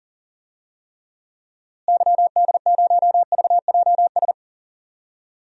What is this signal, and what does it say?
Morse code, heard as a keyed tone, about 40 wpm, 700 Hz. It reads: YD0VJS
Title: DXCC-100 Call Signs - 6 Characters - Repeated 2x - 40wpm